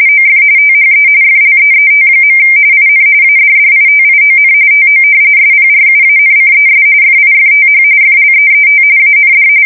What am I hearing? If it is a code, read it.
RTTY, about 45.45 baud. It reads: OF THE FLEET. IN ADDITION TO SPRUANCES FIFTH FLEET AND
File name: fsk170 / battleshipnj